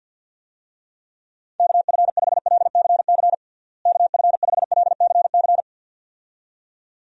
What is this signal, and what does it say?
Morse code, heard as a keyed tone, about 48 wpm, 700 Hz. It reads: KF5LCC KF5LCC
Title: USA Call Signs - 6 Characters - Repeated 2x - 50wpm